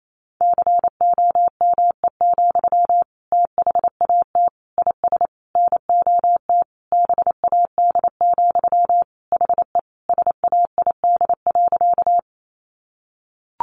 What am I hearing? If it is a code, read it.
Morse code, heard as a keyed tone, about 28 wpm, 700 Hz. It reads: COME, THAT IS NOT BAD, HE SAID.
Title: Sentences from Top 700 Words 28wpm